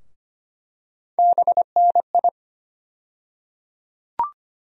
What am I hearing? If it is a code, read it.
Morse code, heard as a keyed tone, about 25 wpm, 700 Hz. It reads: BNI
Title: USA Call Sign Postfixes - 3 Letters 25wpm